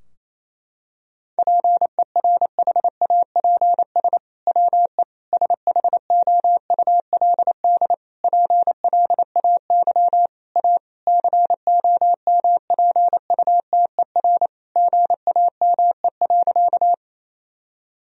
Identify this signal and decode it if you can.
Morse code, heard as a keyed tone, about 28 wpm, 700 Hz. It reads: PERHAPS WE SHOULD PLAY A COMPUTER GAME.
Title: Sentences from Top 1000 Words - Review Part 3 - 28wpm